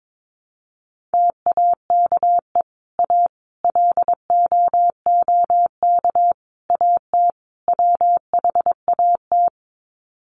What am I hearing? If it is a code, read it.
Morse code, heard as a keyed tone, about 22 wpm, 700 Hz. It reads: TAKE A LOOK AT WHAT
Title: Sets of 5 Words Encore2 22wpm